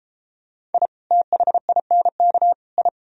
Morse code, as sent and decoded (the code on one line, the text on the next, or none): .. / - .... .. -. -.- / ..
I THINK I